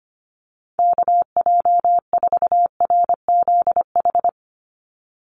Morse code, heard as a keyed tone, about 25 wpm, 700 Hz. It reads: KJ4RZH
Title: USA Call Signs 25wpm